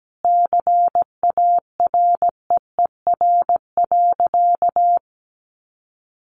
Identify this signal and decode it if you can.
Morse code, heard as a keyed tone, about 17 wpm, 700 Hz. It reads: CAREER.
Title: Sentences from Top 1200 Words 17wpm